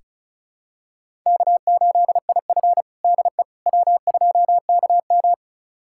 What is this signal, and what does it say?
Morse code, heard as a keyed tone, about 36 wpm, 700 Hz. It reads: K8IF DE W2KM
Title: USA Callsign DE Callsign - 4 Characters - 35wpm